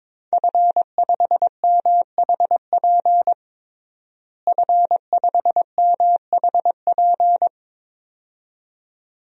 Morse code, heard as a keyed tone, about 21 wpm, 700 Hz. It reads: F5MHP F5MHP
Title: DXCC-100 Call Signs - 5 Characters - Repeated 2x - 22wpm